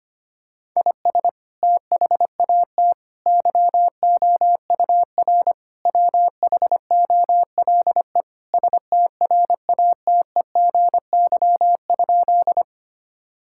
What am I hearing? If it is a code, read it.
Morse code, heard as a keyed tone, about 25 wpm, 700 Hz. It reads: IS THAT YOUR WHOLE STRATEGY?
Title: Sentences from Top 1000 Words - Review Part 5 - 25wpm